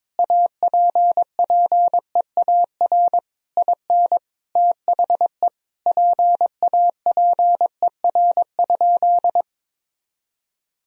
Morse code, heard as a keyed tone, about 22 wpm, 700 Hz. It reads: APPEAR IN THE PAPER?
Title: Sentences from Top 1000 Words - Review Part 2 - 22wpm